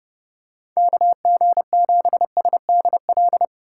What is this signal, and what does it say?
Morse code, heard as a keyed tone, about 26 wpm, 700 Hz. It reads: KG7SDL